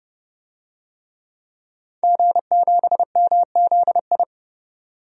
Morse code, heard as a keyed tone, about 30 wpm, 700 Hz. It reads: G7MZI
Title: DXCC-100 Call Signs - 5 Characters - 30wpm